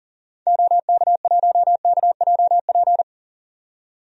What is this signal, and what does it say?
Morse code, heard as a keyed tone, about 40 wpm, 700 Hz. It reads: OK1KJP
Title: DXCC-100 Call Signs - 6 Characters - 40wpm